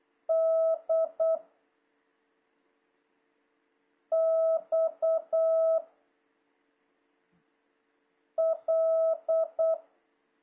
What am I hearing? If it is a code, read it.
Morse code, heard as a keyed tone, about 8 wpm, 650 Hz. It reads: D X L
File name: jh8cbh / 2026cw